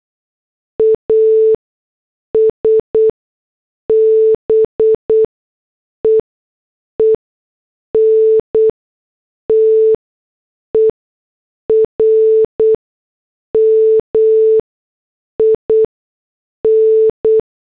Morse code, heard as a keyed tone, about 8 wpm, spaced as for 6 wpm, 440 Hz. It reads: AS BEEN TERMIN